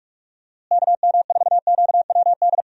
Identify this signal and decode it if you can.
Morse code, heard as a keyed tone, about 42 wpm, 700 Hz. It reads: KM4QWD
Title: USA Call Signs - 6 Characters - Repeated 3x - 45wpm